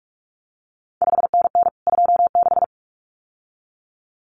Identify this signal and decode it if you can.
Morse code, heard as a keyed tone, about 46 wpm, 700 Hz. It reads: 5NN 26
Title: CQ WW DX Contest Exchange Encore - 45wpm